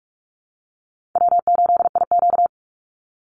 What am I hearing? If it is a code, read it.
Morse code, heard as a keyed tone, about 44 wpm, 700 Hz. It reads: W8IQ
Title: Easy Call Signs Repeated 3x - 45wpm